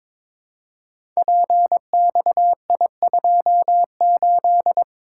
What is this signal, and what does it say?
Morse code, heard as a keyed tone, about 23 wpm, 700 Hz. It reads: PXI28
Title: Code Groups - 5 Letter:Numbers - 22wpm